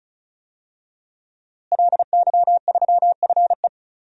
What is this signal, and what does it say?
Morse code, heard as a keyed tone, about 36 wpm, 700 Hz. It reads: LY3FE
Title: DXCC-100 Call Signs - 5 Characters - Repeated 3x - 35wpm